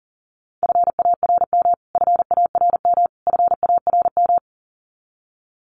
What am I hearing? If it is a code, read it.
Morse code, heard as a keyed tone, about 40 wpm, 700 Hz. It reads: FARM FARM FARM
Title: Top 500 Words Repeated 40wpm